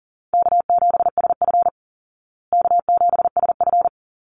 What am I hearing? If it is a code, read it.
Morse code, heard as a keyed tone, about 40 wpm, 700 Hz. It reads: K7SF K7SF
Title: Easy Call Signs Repeated 2x - 40wpm